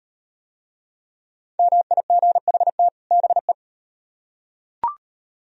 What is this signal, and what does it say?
Morse code, heard as a keyed tone, about 38 wpm, 700 Hz. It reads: MIGHT BE